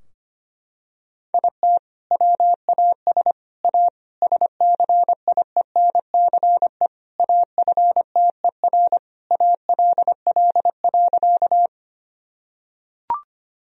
Morse code, heard as a keyed tone, about 25 wpm, 700 Hz. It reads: IT WAS A SCIENCE AFTER ALL.
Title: Sentences from Top 1000 Words - Review Part 2 - 25wpm